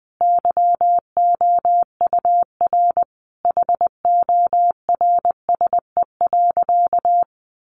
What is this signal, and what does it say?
Morse code, heard as a keyed tone, about 20 wpm, 700 Hz. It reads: YOUR HORSE.
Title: Sentences from Top 1500 Words 20wpm